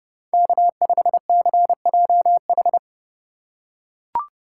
Morse code, heard as a keyed tone, about 28 wpm, 700 Hz. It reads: K5CJH